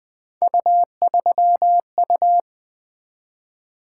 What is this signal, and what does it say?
Morse code, heard as a keyed tone, about 20 wpm, 700 Hz. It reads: U3U
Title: Three Letter-Number Combinations - 20wpm